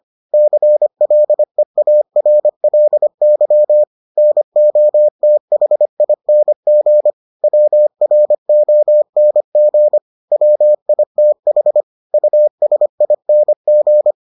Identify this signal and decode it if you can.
Morse code, heard as a keyed tone, about 25 wpm, 600 Hz. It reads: CLEARLY NOTHING WRONG WITH USING